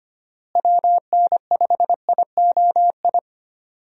Morse code, heard as a keyed tone, about 25 wpm, 700 Hz. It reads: WN5IOI
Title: USA Call Signs - 6 Characters - Repeated 2x - 25wpm